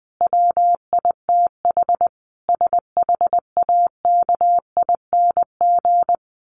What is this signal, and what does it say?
Morse code, heard as a keyed tone, about 20 wpm, 700 Hz. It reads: WITH SHAKING
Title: A Sound Of Thunder 20wpm